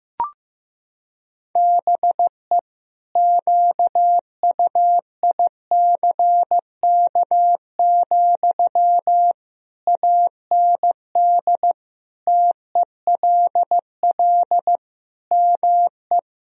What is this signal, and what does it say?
Morse code, heard as a keyed tone, about 15 wpm, 700 Hz. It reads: BE QUICK, AND TELL ME
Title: Sentences from Top 1000 Words - Review Part 3 - 15wpm